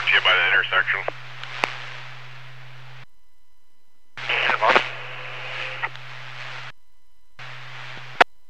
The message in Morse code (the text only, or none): none